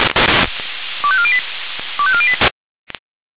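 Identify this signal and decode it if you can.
Five-tone selective call (ZVEI), heard as quick runs of five tones, tones of about 70 ms. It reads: 25408 25408
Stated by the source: Five-tone ZVEI selective call received on SDR.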